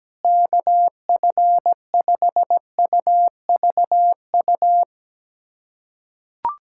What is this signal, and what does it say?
Morse code, heard as a keyed tone, about 17 wpm, 700 Hz. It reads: KF5UVU
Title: USA Callsign DE Callsign - 6 Characters - 17wpm